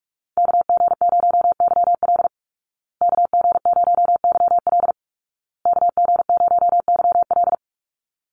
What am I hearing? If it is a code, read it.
Morse code, heard as a keyed tone, about 44 wpm, 700 Hz. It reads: KG0YL KG0YL KG0YL
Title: Call Signs Repeated 2x - 45wpm